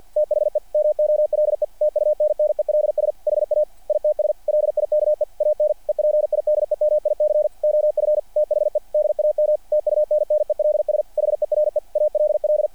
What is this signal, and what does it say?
Morse code, heard as a keyed tone, about 48 wpm, 600 Hz. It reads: THE MOLE TUNNELS HA ITS LIKE AN EPIDEMIC OF THE DAM TUNNELS HERE ALL
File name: archived / CRSnet